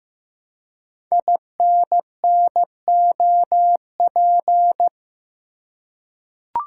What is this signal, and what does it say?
Morse code, heard as a keyed tone, about 15 wpm, 700 Hz. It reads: INNOP
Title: Code Groups - 5 Letters - 15wpm